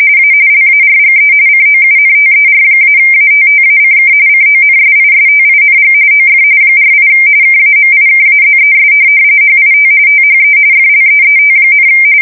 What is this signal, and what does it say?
RTTY, about 45.45 baud. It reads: A LARGE COUNTER-ATTACK CONSISTING OF 48 KAMIKAZES HIT THE FLEET,
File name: fsk170 / USS-Massachusetts